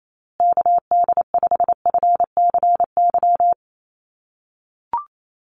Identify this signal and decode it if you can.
Morse code, heard as a keyed tone, about 28 wpm, 700 Hz. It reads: KD5FCY
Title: USA Call Signs Repeated 2x 28wpm